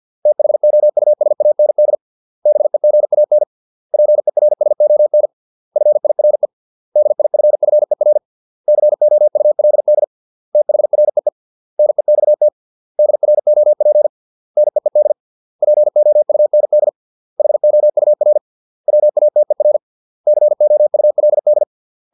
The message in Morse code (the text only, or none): THOUSAND BEGAN PERSON FIRE DIFFER COULD THREE NEXT DROP NEED POUND HOUR WATER COULD